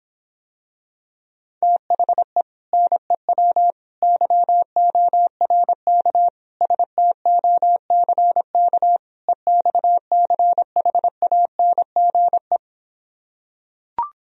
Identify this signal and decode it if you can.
Morse code, heard as a keyed tone, about 26 wpm, 700 Hz. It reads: THE NEW YORK STOCK EXCHANGE